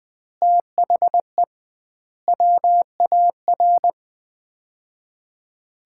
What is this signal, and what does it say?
Morse code, heard as a keyed tone, about 20 wpm, 700 Hz. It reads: THE WAR
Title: 2x Word-Spacing - Sets of 5 Words - 20wpm